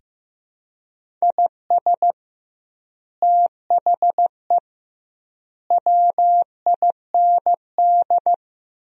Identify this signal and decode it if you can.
Morse code, heard as a keyed tone, about 15 wpm, 700 Hz. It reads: IS THE WIND
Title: 2x Word-Spacing - Sentences from Top 400 Words - 15wpm